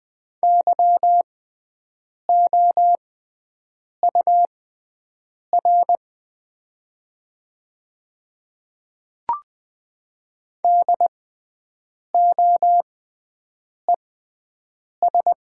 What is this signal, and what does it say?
Morse code, heard as a keyed tone, about 20 wpm, spaced as for 7 wpm, 690 Hz. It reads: YOUR DOES
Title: 6x Character Spacing - Top 200 Words - 20wpm